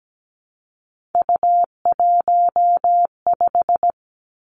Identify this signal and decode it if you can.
Morse code, heard as a keyed tone, about 17 wpm, 700 Hz. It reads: U15
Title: Three Challenging Letter-Number Combinations 17wpm